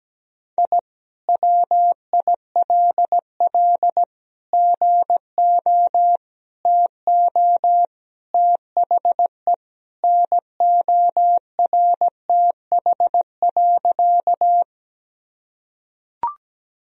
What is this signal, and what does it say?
Morse code, heard as a keyed tone, about 17 wpm, 700 Hz. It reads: I WILL GO TO THE NORTH.
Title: Sentences from Top 1000 Words - Review Part 1 - 17wpm